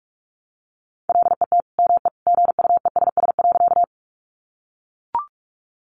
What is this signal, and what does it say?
Morse code, heard as a keyed tone, about 44 wpm, 700 Hz. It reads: LET ME GUESS.
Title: Sentences from Top 1500 Words 45wpm